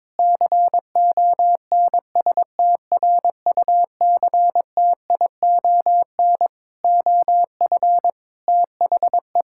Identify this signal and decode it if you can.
Morse code, heard as a keyed tone, about 22 wpm, 700 Hz. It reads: CONSTRUCTION OF THE
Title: Sentences from Top 1600 Words 22wpm